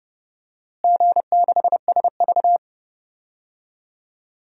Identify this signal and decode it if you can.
Morse code, heard as a keyed tone, about 30 wpm, 700 Hz. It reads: G6SV
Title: DXCC-100 Call Signs - 4 Characters - Repeated 2x - 30wpm